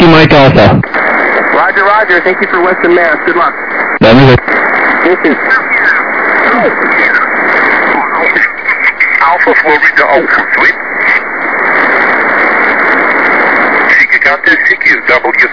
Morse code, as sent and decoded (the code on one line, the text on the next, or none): none